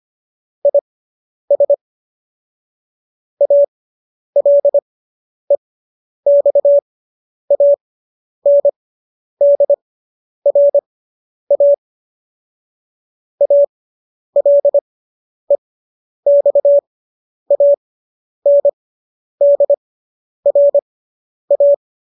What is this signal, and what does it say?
Morse code, heard as a keyed tone, about 25 wpm, spaced as for 10 wpm, 570 Hz. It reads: IS ALEXANDRA ALEXANDRA
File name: qso219 10